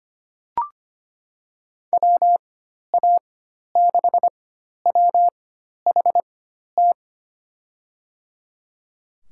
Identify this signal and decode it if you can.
Morse code, heard as a keyed tone, about 25 wpm, spaced as for 12 wpm, 700 Hz. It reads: WA6WHT
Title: USA Call Signs - 6 Characters - 4x Character Spacing - 25wpm